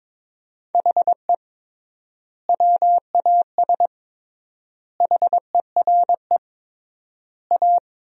Morse code, heard as a keyed tone, about 22 wpm, 700 Hz. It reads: HE WAS HERE A
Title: 3x Word-Spacing - Sentences from Top 500 Words - 22wpm